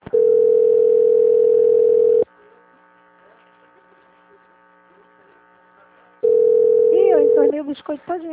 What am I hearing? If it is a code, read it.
Ringback tone (North America), 440 + 480 Hz, with speech.
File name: out- - - - - .34099